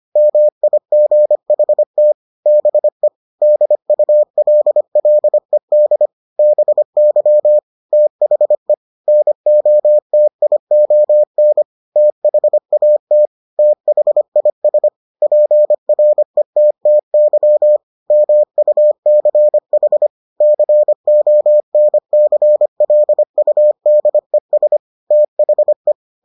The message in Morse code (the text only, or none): MIGHT BE DULLED BY THE NOTION THAT THIS PRETTY MUCH CONCLUDES THE